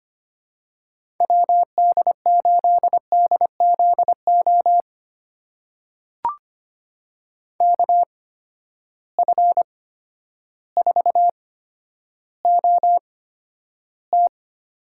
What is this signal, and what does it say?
Morse code, heard as a keyed tone, about 25 wpm, spaced as for 11 wpm, 700 Hz. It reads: WD8DZO KF4OT
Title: USA Call Signs - 6 Characters - 8x Character Spacing - 25wpm